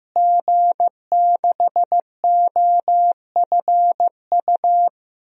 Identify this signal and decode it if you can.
Morse code, heard as a keyed tone, about 16 wpm, 700 Hz. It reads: G6OFU
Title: DXCC-100 Call Signs - 5 Characters - Repeated 2x - 15wpm